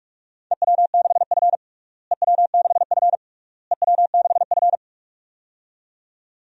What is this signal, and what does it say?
Morse code, heard as a keyed tone, about 46 wpm, 700 Hz. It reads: EW6F EW6F EW6F
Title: DXCC-100 Call Signs - 4 Characters - Repeated 3x - 45wpm